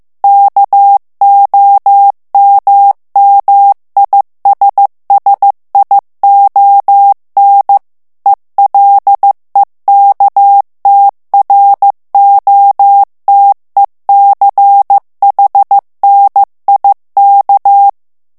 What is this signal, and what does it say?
Morse code, heard as a keyed tone, about 15 wpm, 800 Hz. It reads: KOMMISSION ELEKTROTECHNIK
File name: ALLEMAND texte clair 1 vitesse 14 mots minute